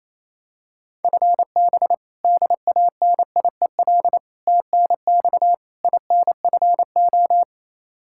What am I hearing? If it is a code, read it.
Morse code, heard as a keyed tone, about 28 wpm, 700 Hz. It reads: FB DANIEL TNX INFO
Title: Practice QSO - Exchange 3 - More Info - 28wpm